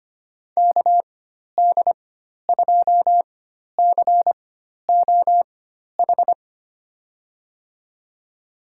Morse code, heard as a keyed tone, about 25 wpm, spaced as for 12 wpm, 700 Hz. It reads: KD2COH
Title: USA Call Signs - 4x Character Spacing - 25wpm